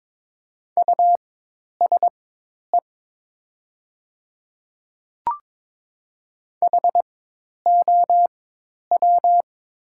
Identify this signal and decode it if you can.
Morse code, heard as a keyed tone, about 22 wpm, spaced as for 10 wpm, 700 Hz. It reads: USE HOW
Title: 4x Character Spacing - Top 500 Words - 22wpm